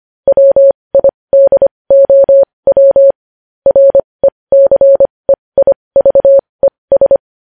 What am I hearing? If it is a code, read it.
Morse code, heard as a keyed tone, about 25 wpm, spaced as for 20 wpm, 550 Hz. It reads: WIDOW RECEIVES
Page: news-headlines-in-morse-code-at-20-wpm